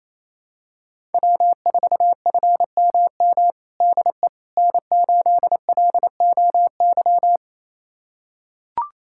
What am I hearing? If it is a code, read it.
Morse code, heard as a keyed tone, about 27 wpm, 700 Hz. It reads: W4FMM DE N8LOY
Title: USA Callsign DE Callsign - 5 Characters - 28wpm